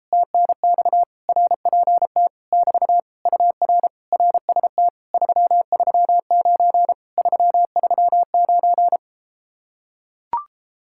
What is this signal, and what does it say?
Morse code, heard as a keyed tone, about 33 wpm, 700 Hz. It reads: TNX RPT <BT> UR RST 339 339